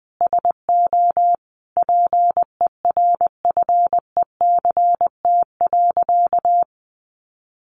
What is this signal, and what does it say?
Morse code, heard as a keyed tone, about 20 wpm, 700 Hz. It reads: SO PERFECT.
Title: Sentences from Top 1300 Words 20wpm